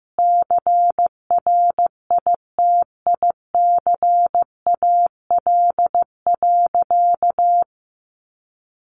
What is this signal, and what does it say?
Morse code, heard as a keyed tone, about 15 wpm, 700 Hz. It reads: CRITICAL.
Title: Sentences from Top 1100 Words 15wpm